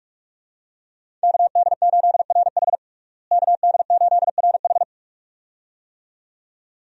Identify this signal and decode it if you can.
Morse code, heard as a keyed tone, about 42 wpm, 700 Hz. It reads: KD8RH KD8RH
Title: USA Call Signs - 5 Characters - Repeated 2x - 45wpm